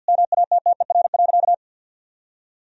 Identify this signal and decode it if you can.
Morse code, heard as a keyed tone, about 46 wpm, 700 Hz. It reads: MATTER.
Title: Sentences from Top 1000 Words - Review Part 3 - 50wpm